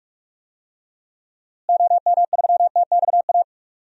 Morse code, heard as a keyed tone, about 46 wpm, 700 Hz. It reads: OM3TXA
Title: DXCC-100 Call Signs - 6 Characters - Repeated 2x - 45wpm